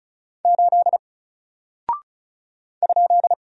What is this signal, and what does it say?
Morse code, heard as a keyed tone, about 36 wpm, 700 Hz. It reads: 8 ?
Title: Single Character (Letter:Number:Punctuation:Prosign) Alt 35wpm